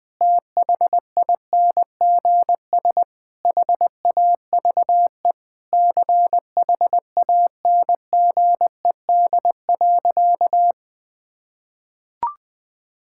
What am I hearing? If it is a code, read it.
Morse code, heard as a keyed tone, about 21 wpm, 700 Hz. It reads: THINGS HAVE CHANGED.
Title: Sentences with 3-Word Phrases - 20wpm